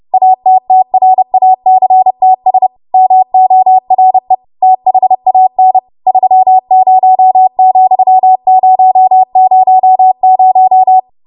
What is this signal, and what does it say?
Morse code, heard as a keyed tone, about 30 wpm, 750 Hz. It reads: ATTRACTS MORE THAN 30,000
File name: Morse / 30WPM